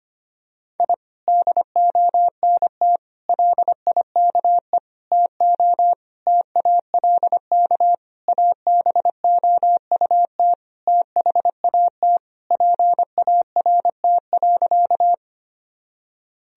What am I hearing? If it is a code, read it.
Morse code, heard as a keyed tone, about 25 wpm, 700 Hz. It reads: I DONT LIKE TO TALK ABOUT THAT PART.
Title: Sentences from Top 1000 Words - Review Part 1 - 25wpm